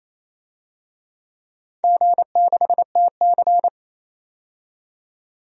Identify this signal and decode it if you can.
Morse code, heard as a keyed tone, about 28 wpm, 700 Hz. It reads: G6TC
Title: DXCC-100 Call Signs - 4 Characters - 28wpm